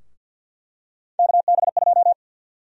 Morse code, heard as a keyed tone, about 51 wpm, 700 Hz. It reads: KB2
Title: USA Call Sign Prefixes - Repeated 2x - 50wpm